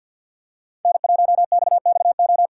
Morse code, heard as a keyed tone, about 52 wpm, 700 Hz. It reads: N1XXO
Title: USA Call Signs - 5 Characters - 3x Character Spacing - 50wpm